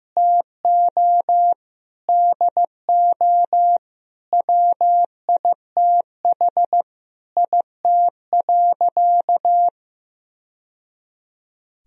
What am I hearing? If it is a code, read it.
Morse code, heard as a keyed tone, about 15 wpm, 700 Hz. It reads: TO DO WITH IT.
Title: Sentences from Top 1000 Words - Review Part 1 - 15wpm